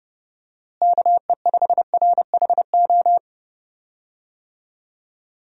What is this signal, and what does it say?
Morse code, heard as a keyed tone, about 29 wpm, 700 Hz. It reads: KE5RHO
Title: USA Call Signs - 6 Characters - Repeated 3x - 30wpm